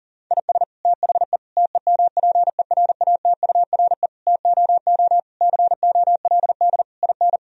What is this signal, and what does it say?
Morse code, heard as a keyed tone, about 40 wpm, 700 Hz. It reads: IS THE TEMPERATURE TOO COLD IN
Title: Sentences from Top 1000 Words - Review Part 4 - 40wpm